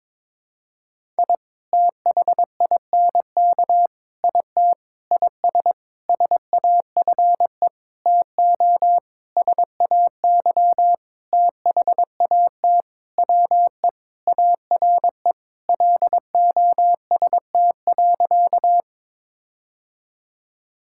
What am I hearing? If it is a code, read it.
Morse code, heard as a keyed tone, about 22 wpm, 700 Hz. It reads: I THINK IT IS SAFE TO SAY THAT WE ARE LOST.
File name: practice / 22wpm / Sentences with 7-Word Phrases - 22wpm